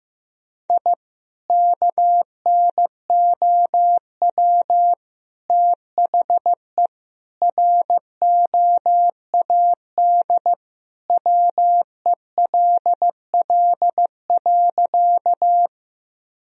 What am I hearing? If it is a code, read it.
Morse code, heard as a keyed tone, about 16 wpm, 700 Hz. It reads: I KNOW THE ROAD WELL.
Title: Sentences from Top 1000 Words - Review Part 2 - 15wpm